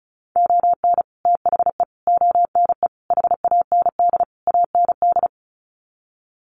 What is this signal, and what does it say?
Morse code, heard as a keyed tone, about 34 wpm, 700 Hz. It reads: ON THE ONE HAND AND
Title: Sets of 5 Words Encore2 35wpm